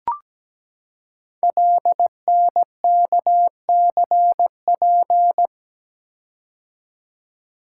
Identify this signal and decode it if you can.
Morse code, heard as a keyed tone, about 17 wpm, 700 Hz. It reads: LNKCP
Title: Code Groups - 5 Letter:Numbers - 17wpm